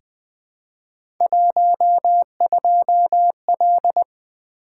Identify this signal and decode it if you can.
Morse code, heard as a keyed tone, about 20 wpm, 700 Hz. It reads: 12L
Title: Three Letter-Number Combinations - 20wpm